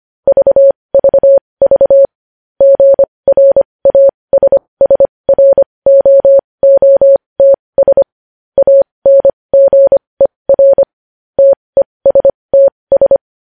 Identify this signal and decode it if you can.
Morse code, heard as a keyed tone, about 25 wpm, spaced as for 20 wpm, 550 Hz. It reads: VVV GRASSROOTS ANGER TESTS
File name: podcast / News headlines-1x-25x20